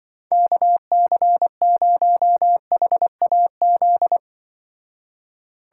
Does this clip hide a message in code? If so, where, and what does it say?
Yes: Morse code from 0.3 to 4.2 s, heard as a keyed tone, about 25 wpm, 700 Hz. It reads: KC0HAZ